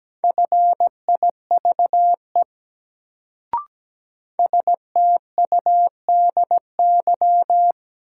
Morse code, heard as a keyed tone, about 17 wpm, 700 Hz. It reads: FIVE STUDY
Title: Sets of 5 Words - New Word Drill - 17wpm